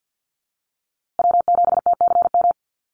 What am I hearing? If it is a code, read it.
Morse code, heard as a keyed tone, about 50 wpm, 700 Hz. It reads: W7TCM
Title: USA Call Signs Repeated 2x 50wpm